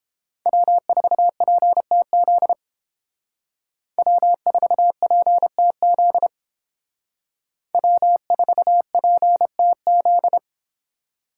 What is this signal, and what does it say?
Morse code, heard as a keyed tone, about 29 wpm, 700 Hz. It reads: W4PTZ W4PTZ W4PTZ